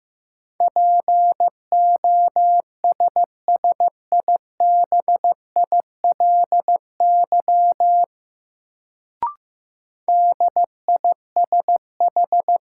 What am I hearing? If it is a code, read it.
Morse code, heard as a keyed tone, about 15 wpm, 700 Hz. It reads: POSSIBILY DISH
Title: Sentences from Top 1300 Words - New Word Drill Alt - 15wpm